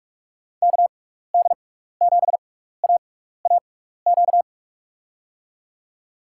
Morse code, heard as a keyed tone, about 41 wpm, spaced as for 15 wpm, 690 Hz. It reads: KD7AAQ
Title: USA Call Signs - 6x Character Spacing - 45wpm